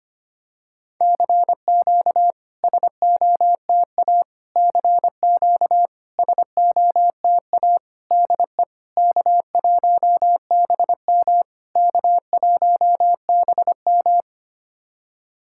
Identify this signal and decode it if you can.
Morse code, heard as a keyed tone, about 25 wpm, 700 Hz. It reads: CQ SOTA CQ SOTA DE K1BM K1BM
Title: SOTA QSO Practice 25wpm